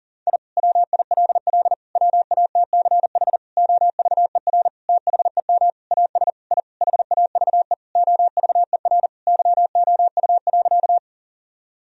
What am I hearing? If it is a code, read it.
Morse code, heard as a keyed tone, about 40 wpm, 700 Hz. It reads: I WILL WATCH OVER THEM AS I HAVE OVER YOU.
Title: Sentences from Top 1000 Words - Review Part 1 - 40wpm